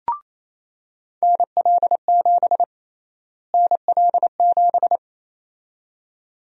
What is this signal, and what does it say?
Morse code, heard as a keyed tone, about 28 wpm, 700 Hz. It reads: NL7 NL7
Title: USA Call Sign Prefixes - Repeated 2x - 28wpm